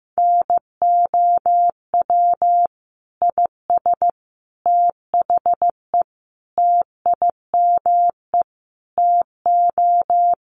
Morse code, heard as a keyed tone, about 15 wpm, 700 Hz. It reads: NOW IS THE TIME TO
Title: Sets of 5 Words Encore2 15wpm